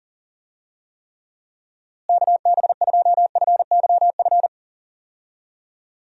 Morse code, heard as a keyed tone, about 40 wpm, 700 Hz. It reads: KB2FYF
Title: USA Call Signs - 6 Characters - Repeated 3x - 40wpm